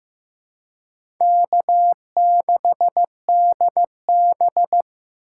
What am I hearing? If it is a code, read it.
Morse code, heard as a keyed tone, about 15 wpm, 700 Hz. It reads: K6DB
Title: USA Call Signs - 4 Characters - 3x Character Spacing - 15wpm